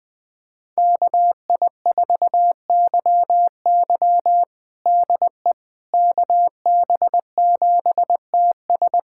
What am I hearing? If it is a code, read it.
Morse code, heard as a keyed tone, about 20 wpm, 700 Hz. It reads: KI4YY DE KB7TS
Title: USA Callsign DE Callsign - 5 Characters - 20wpm